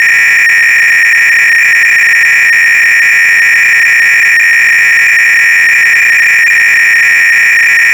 300-baud modem data.